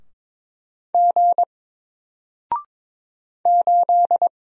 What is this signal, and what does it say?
Morse code, heard as a keyed tone, about 22 wpm, 700 Hz. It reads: G 8
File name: Single-Letter-Number-22wpm